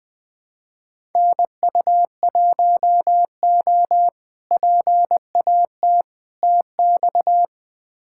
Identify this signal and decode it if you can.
Morse code, heard as a keyed tone, about 20 wpm, 700 Hz. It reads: NU1O PAT TX
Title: K1USN SST QSO Practice - 20wpm